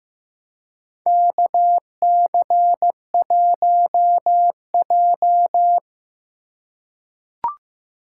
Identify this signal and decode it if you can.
Morse code, heard as a keyed tone, about 15 wpm, 700 Hz. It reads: KC1J
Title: USA Call Signs - 4 Characters - 4x Character Spacing - 15wpm